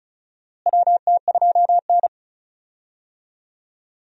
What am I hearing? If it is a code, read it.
Morse code, heard as a keyed tone, about 36 wpm, 700 Hz. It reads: WT2N
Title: USA Call Signs - 4 Characters - Repeated 2x - 35wpm